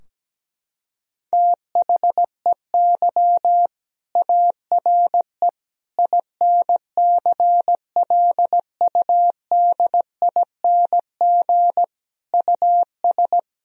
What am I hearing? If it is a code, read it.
Morse code, heard as a keyed tone, about 17 wpm, 700 Hz. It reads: THEY ARE INCLUDING US
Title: Sentences from Top 1000 Words - Review Part 4 - 17wpm